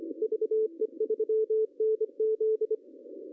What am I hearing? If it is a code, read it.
Morse code, heard as a keyed tone, about 25 wpm, 440 Hz. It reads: VE3NZ
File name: CW / cqww-cw-k5tr